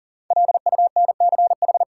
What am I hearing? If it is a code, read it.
Morse code, heard as a keyed tone, about 40 wpm, 700 Hz. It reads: LUNCH
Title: Five Letter Words - 40wpm